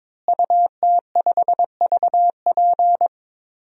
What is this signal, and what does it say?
Morse code, heard as a keyed tone, about 22 wpm, 700 Hz. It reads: UT5VP
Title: DXCC-100 Call Signs - 5 Characters - Repeated 3x - 22wpm